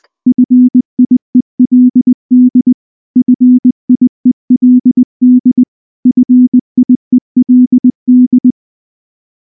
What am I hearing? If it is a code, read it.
Morse code, heard as a keyed tone, about 20 wpm, 250 Hz. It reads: FIELD FIELD FIELD
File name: morse / morse-training-20WPM-7